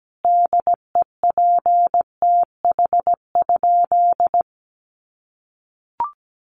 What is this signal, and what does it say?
Morse code, heard as a keyed tone, about 17 wpm, 700 Hz. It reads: DEPTH?
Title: Sentences from Top 1200 Words 17wpm